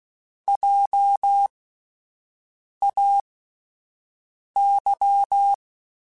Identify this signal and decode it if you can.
Morse code, heard as a keyed tone, about 16 wpm, spaced as for 5 wpm, 800 Hz. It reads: JAY